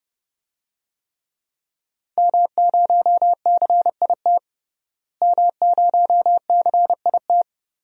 Morse code, heard as a keyed tone, about 32 wpm, 700 Hz. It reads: M0CIT M0CIT
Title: DXCC-100 Call Signs - 5 Characters - Repeated 3x - 30wpm